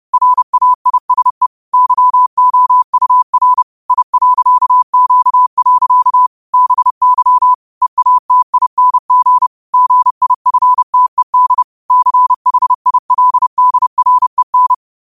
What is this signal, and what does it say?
Morse code, heard as a keyed tone, about 30 wpm, 1000 Hz. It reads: RAISE YOUR I.Q. BY EATING GIFTED CHILDREN
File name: 30wpm